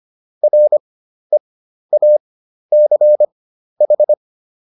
Morse code, heard as a keyed tone, about 25 wpm, spaced as for 12 wpm, 600 Hz. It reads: REACH